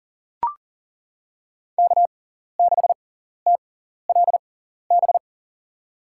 Morse code, heard as a keyed tone, about 40 wpm, spaced as for 14 wpm, 700 Hz. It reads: K6TLB
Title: USA Call Signs - 5 Characters - 6x Character Spacing - 40wpm